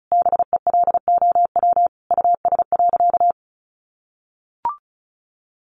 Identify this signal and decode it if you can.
Morse code, heard as a keyed tone, about 34 wpm, 700 Hz. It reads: BELOW US.
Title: Sentences from Top 1400 Words 35wpm